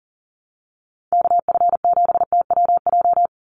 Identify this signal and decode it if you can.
Morse code, heard as a keyed tone, about 37 wpm, 700 Hz. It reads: KF7TWJ
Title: USA Call Signs 40wpm